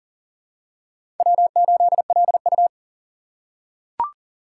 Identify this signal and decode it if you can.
Morse code, heard as a keyed tone, about 39 wpm, 700 Hz. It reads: W8LU
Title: USA Call Signs - 4 Characters - 8x Character Spacing - 40wpm